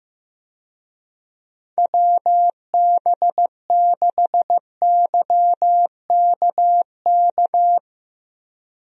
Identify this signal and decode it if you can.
Morse code, heard as a keyed tone, about 15 wpm, 700 Hz. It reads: WB6YKK
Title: USA Call Signs - 6 Characters - 15wpm